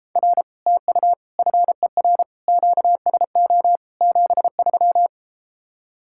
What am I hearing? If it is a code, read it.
Morse code, heard as a keyed tone, about 33 wpm, 700 Hz. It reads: R TU FER QSO 73